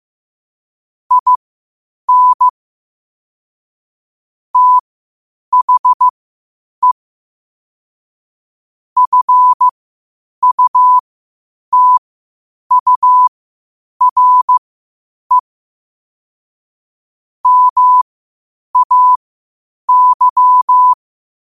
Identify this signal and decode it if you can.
Morse code, heard as a keyed tone, about 15 wpm, spaced as for 8 wpm, 1000 Hz. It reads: IN THE FUTURE MAY
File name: qotdfiles / 8wpm